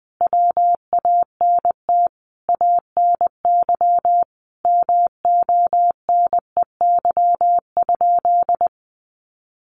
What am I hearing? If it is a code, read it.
Morse code, heard as a keyed tone, about 20 wpm, 700 Hz. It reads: WANT ANY MONEY?
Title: Sentences from Top 400 Words 20wpm